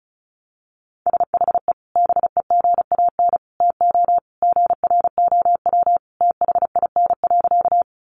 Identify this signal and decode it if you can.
Morse code, heard as a keyed tone, about 34 wpm, 700 Hz. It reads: SHE BEGAN TO GROW THIN.
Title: Sentences from Top 1200 Words 35wpm